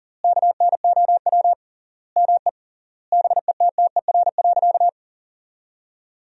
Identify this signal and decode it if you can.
Morse code, heard as a keyed tone, about 40 wpm, 700 Hz. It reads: KNOW ME BETTER.
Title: 3x Word-Spacing - Sentences from Top 400 Words - 40wpm